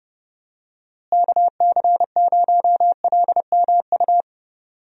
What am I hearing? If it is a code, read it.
Morse code, heard as a keyed tone, about 32 wpm, 700 Hz. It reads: KC0LMU
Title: USA Call Signs - 6 Characters - Repeated 3x - 30wpm